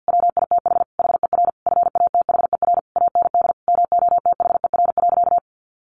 Morse code, heard as a keyed tone, about 50 wpm, 700 Hz. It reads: WITH HER FATHER AND MOTHER.
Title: Sentences from Top 1100 Words 50wpm